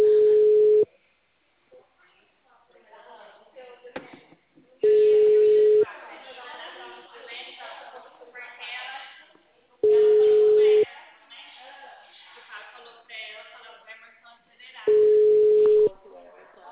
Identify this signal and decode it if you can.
Ringback tone (Europe, 425 Hz), with speech.